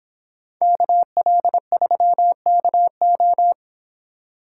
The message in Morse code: KL3KO

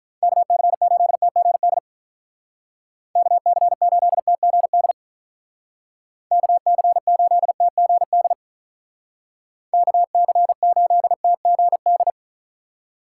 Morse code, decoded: KC8TGD KC8TGD KC8TGD KC8TGD

